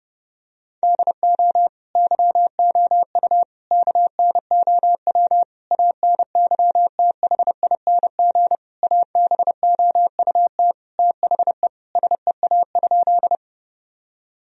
Morse code, heard as a keyed tone, about 32 wpm, 700 Hz. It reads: DO YOU KNOW ANYTHING ABOUT THE SEA?
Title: Sentences from Top 1000 Words - Review Part 1 - 30wpm